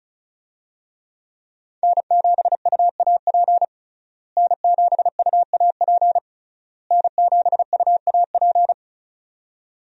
Morse code, heard as a keyed tone, about 34 wpm, 700 Hz. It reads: N7UAP N7UAP N7UAP
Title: USA Call Signs - 5 Characters - Repeated 3x - 35wpm